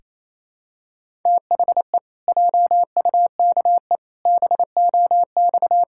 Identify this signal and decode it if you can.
Morse code, heard as a keyed tone, about 28 wpm, 700 Hz. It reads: THE JUKE BOX
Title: Pangram Challenge v2 - 28wpm